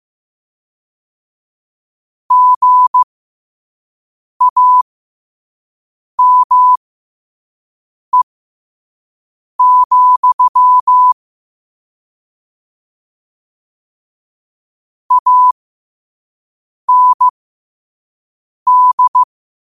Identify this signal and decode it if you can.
Morse code, heard as a keyed tone, about 15 wpm, spaced as for 5 wpm, 1000 Hz. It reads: GAME, AND
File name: qotdfiles / 5wpm